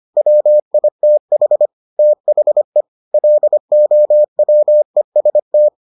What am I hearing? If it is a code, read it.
Morse code, heard as a keyed tone, about 25 wpm, 600 Hz. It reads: WITH THE LOWEST